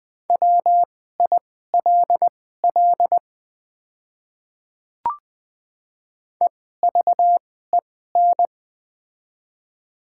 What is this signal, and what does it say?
Morse code, heard as a keyed tone, about 20 wpm, spaced as for 14 wpm, 700 Hz. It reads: WILL EVEN
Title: 2x Character Spacing - Top 300 Words - 20wpm